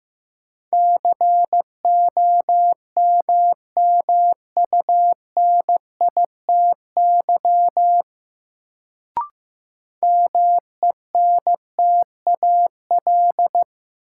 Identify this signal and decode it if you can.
Morse code, heard as a keyed tone, about 15 wpm, 700 Hz. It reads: COMMUNITY MENTAL
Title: Sentences from Top 1000 Words - Review Part 4 - New Word Drill - 15wpm